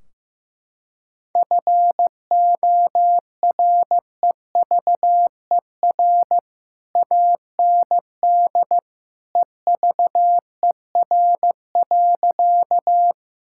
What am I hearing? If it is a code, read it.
Morse code, heard as a keyed tone, about 16 wpm, 700 Hz. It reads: FOREVER AND EVER.
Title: Sentences from Top 1600 Words 15wpm